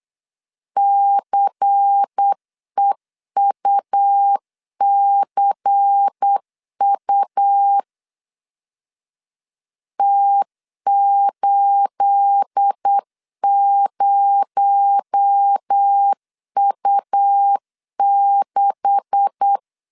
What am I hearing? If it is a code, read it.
Morse code, heard as a keyed tone, about 8 wpm, 800 Hz. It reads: CEUCU T80U6
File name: CW / DET / 1 9 1500C S A 8wpm 5637